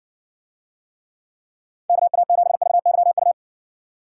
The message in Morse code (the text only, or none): KA7VYU